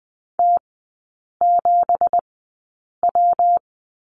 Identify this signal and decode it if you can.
Morse code, heard as a keyed tone, about 20 wpm, spaced as for 8 wpm, 690 Hz. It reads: T7W
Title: 166 Lesson 27e - Number 7 - Three Characters 20wpm